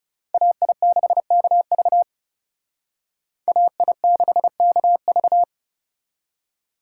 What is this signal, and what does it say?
Morse code, heard as a keyed tone, about 38 wpm, 700 Hz. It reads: AI6KV AI6KV